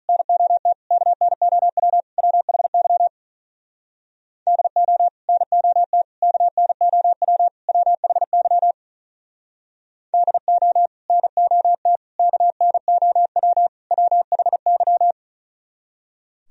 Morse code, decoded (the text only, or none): NOT KNOW WHY DO NOT KNOW WHY DO NOT KNOW WHY